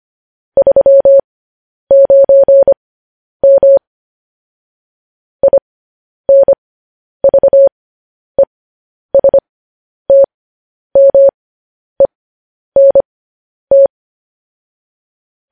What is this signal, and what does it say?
Morse code, heard as a keyed tone, about 25 wpm, spaced as for 10 wpm, 550 Hz. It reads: 39M INVESTMENT